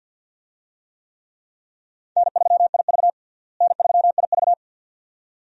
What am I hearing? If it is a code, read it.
Morse code, heard as a keyed tone, about 49 wpm, 700 Hz. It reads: N3IV N3IV
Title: USA Call Signs - 4 Characters - Repeated 2x - 50wpm